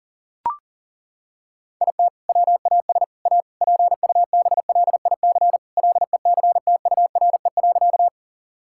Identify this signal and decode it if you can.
Morse code, heard as a keyed tone, about 37 wpm, 700 Hz. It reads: IT WAS A PUBLIC LECTURE.
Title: Sentences from Top 1600 Words 40wpm